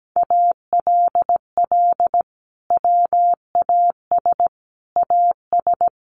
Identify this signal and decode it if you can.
Morse code, heard as a keyed tone, about 17 wpm, 700 Hz. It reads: ALL WAS AS
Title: Sentences from Top 800 Words 17wpm